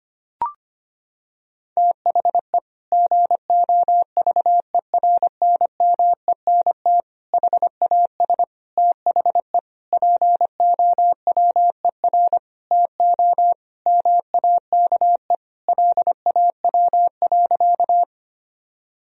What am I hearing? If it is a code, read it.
Morse code, heard as a keyed tone, about 25 wpm, 700 Hz. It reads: THE GOVERNMENT HAS THE POWER TO MAKE LAW.
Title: Sentences from Top 1000 Words - Review Part 3 - 25wpm